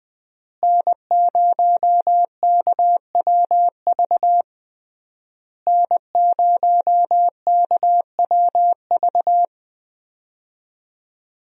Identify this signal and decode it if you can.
Morse code, heard as a keyed tone, about 21 wpm, 700 Hz. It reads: N0KWV N0KWV
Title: USA Call Signs - 5 Characters - Repeated 3x - 20wpm